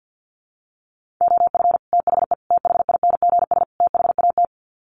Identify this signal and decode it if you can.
Morse code, heard as a keyed tone, about 50 wpm, 700 Hz. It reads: OF THE THINGS THAT
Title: Sets of 4 Words 50wpm